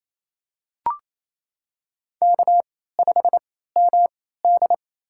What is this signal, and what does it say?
Morse code, heard as a keyed tone, about 28 wpm, spaced as for 16 wpm, 700 Hz. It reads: K5MD
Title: USA Call Signs - 4 Characters - 3x Character Spacing - 28wpm